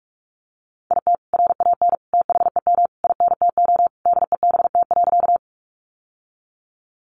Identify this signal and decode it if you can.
Morse code, heard as a keyed tone, about 46 wpm, 700 Hz. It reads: IT RAN THEM INTO DEBT.
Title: Sentences from Top 1300 Words 45wpm